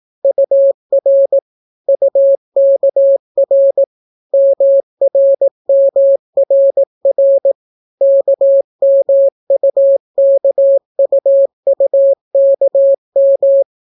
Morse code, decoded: UR UKR MRMRR KMUKUUKM

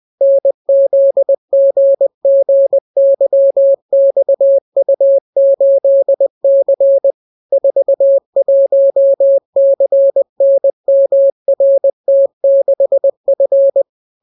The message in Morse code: NZGGYXU8C 41CNMRT6F

